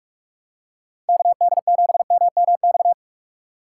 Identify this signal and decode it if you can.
Morse code, heard as a keyed tone, about 46 wpm, 700 Hz. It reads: KD7MMX
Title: USA Call Signs - 6 Characters - 4x Character Spacing - 45wpm